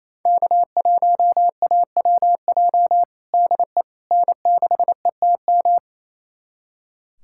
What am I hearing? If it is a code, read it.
Morse code, heard as a keyed tone, about 28 wpm, 700 Hz. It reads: K1AWJ DE N6ETM
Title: USA Callsign DE Callsign 28wpm